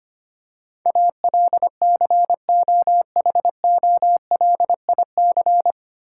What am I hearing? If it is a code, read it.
Morse code, heard as a keyed tone, about 25 wpm, 700 Hz. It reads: ALCOHOLIC
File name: practice / 25wpm / 9 Letter Words - First 3 Letters Given - 25wpm